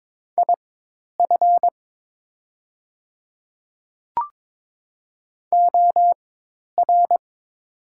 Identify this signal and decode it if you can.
Morse code, heard as a keyed tone, about 22 wpm, spaced as for 10 wpm, 700 Hz. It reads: IF OR
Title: 4x Character Spacing - Top 100 Words - 22wpm